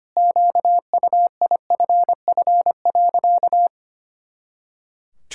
Morse code, heard as a keyed tone, about 25 wpm, 700 Hz. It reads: QUIFF.
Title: Pangram Challenge v2 - 25wpm